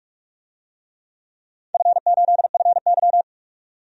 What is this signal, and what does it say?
Morse code, heard as a keyed tone, about 45 wpm, 700 Hz. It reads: F8FY
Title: DXCC-100 Call Signs - 4 Characters - 45wpm